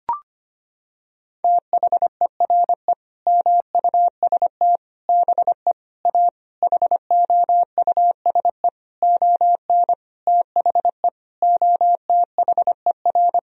Morse code, heard as a keyed tone, about 25 wpm, 700 Hz. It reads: THERE MUST BE A HOUSE ON THE OTHER
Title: Sentences from Top 1000 Words - Review Part 1 - 25wpm